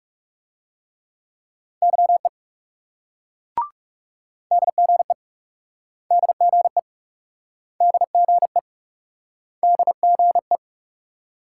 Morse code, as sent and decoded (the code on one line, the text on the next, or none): -.-- . / -.. --. / -.. --. / -.. --. / -.. --. .
YE DGE DGE DGE DGE